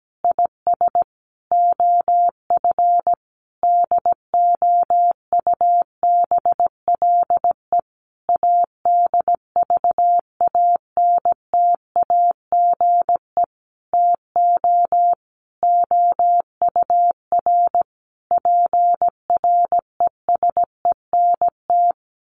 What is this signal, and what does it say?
Morse code, heard as a keyed tone, about 17 wpm, 700 Hz. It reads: IS OF DOUBLE ADVANTAGE TO OUR PRESENT
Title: Sentences from Top 1400 Words 17wpm